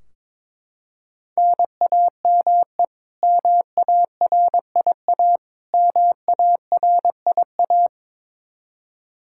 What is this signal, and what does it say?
Morse code, heard as a keyed tone, about 22 wpm, 700 Hz. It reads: NAME MARIA MARIA
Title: Practice QSO - Exchange 1 - Beginning - 22wpm